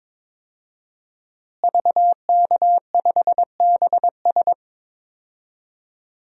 Morse code, heard as a keyed tone, about 21 wpm, 700 Hz. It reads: VK5BS
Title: DXCC-100 Call Signs - 5 Characters - Repeated 2x - 22wpm